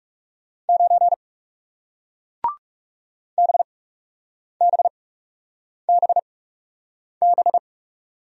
Morse code, heard as a keyed tone, about 37 wpm, spaced as for 8 wpm, 700 Hz. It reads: 9 BBBB
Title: Speed-Racing Single Letter-Number 30wpm